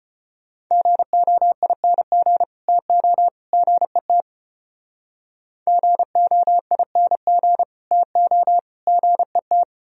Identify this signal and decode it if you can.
Morse code, heard as a keyed tone, about 31 wpm, 700 Hz. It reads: GOING TO GET GOING TO GET
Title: Speed-Racing - Sets of 3 Words 25wpm